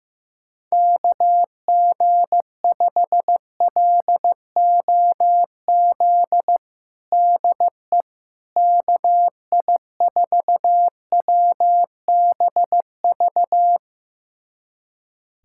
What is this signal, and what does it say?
Morse code, heard as a keyed tone, about 16 wpm, 700 Hz. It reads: KG5LOZ DE KI4WBV
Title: USA Callsign DE Callsign - 6 Characters - 15wpm